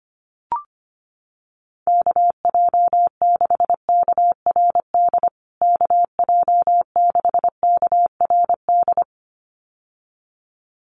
Morse code, heard as a keyed tone, about 25 wpm, 700 Hz. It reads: KJ6KRD KJ6KRD
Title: USA Call Signs Repeated 2x 25wpm